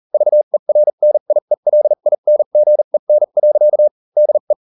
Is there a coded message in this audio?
Yes: Morse code from 0.1 to 4.5 s, heard as a keyed tone, about 40 wpm, spaced as for 35 wpm, 600 Hz. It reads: VERNIELINGEN. DE